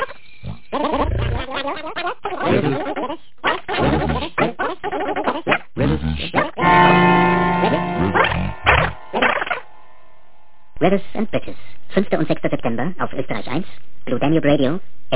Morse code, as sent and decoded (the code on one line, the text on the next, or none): none